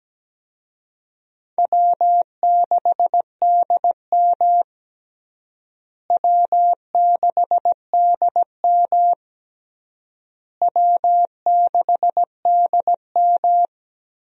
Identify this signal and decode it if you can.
Morse code, heard as a keyed tone, about 17 wpm, 700 Hz. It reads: W6DM W6DM W6DM
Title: USA Call Signs - 4 Characters - Repeated 3x - 17wpm